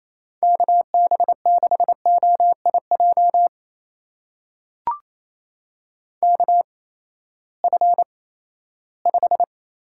Morse code, heard as a keyed tone, about 28 wpm, spaced as for 16 wpm, 700 Hz. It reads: KB6OIJ KF5
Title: USA Call Signs - 6 Characters - 8x Character Spacing - 28wpm